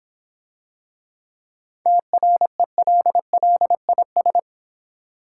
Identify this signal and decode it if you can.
Morse code, heard as a keyed tone, about 26 wpm, 700 Hz. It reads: TRELLIS